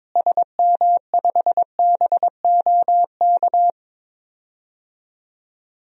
Morse code, heard as a keyed tone, about 22 wpm, 700 Hz. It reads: SM5BOK
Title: DXCC-100 Call Signs - 6 Characters - Repeated 3x - 22wpm